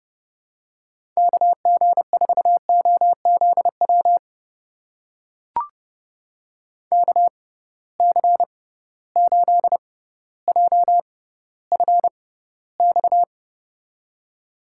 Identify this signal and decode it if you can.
Morse code, heard as a keyed tone, about 30 wpm, spaced as for 15 wpm, 700 Hz. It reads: KG4OZW KC8JFX
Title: USA Call Signs - 6 Characters - 6x Character Spacing - 30wpm